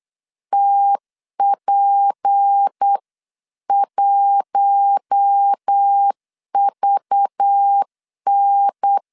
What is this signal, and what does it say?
Morse code, heard as a keyed tone, about 8 wpm, 800 Hz. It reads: TP1VN
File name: CW / DET / 1 6 1500C S B 8wpm 5142